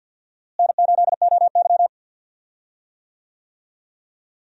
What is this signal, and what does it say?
Morse code, heard as a keyed tone, about 48 wpm, 700 Hz. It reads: N8OY
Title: USA Call Signs - 4 Characters - Repeated 2x - 50wpm